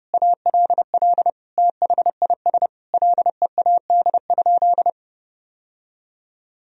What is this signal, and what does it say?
Morse code, heard as a keyed tone, about 31 wpm, 700 Hz. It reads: ALL THIS LEAD?
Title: Sentences from Top 1000 Words - Review Part 2 - 30wpm